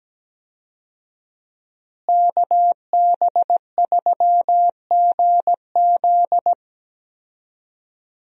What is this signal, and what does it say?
Morse code, heard as a keyed tone, about 17 wpm, 700 Hz. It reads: KB3GZ
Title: USA Call Signs - 5 Characters - Repeated 2x - 17wpm